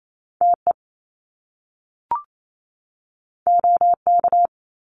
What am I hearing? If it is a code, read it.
Morse code, heard as a keyed tone, about 28 wpm, 700 Hz. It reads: TE OK
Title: Top Two Letter Combinations in Words 28wpm